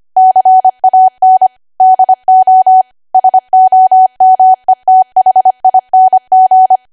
Morse code, heard as a keyed tone, about 25 wpm, 750 Hz. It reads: CAN DO SOMETHING
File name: Archive / 25 WPM / 25WPM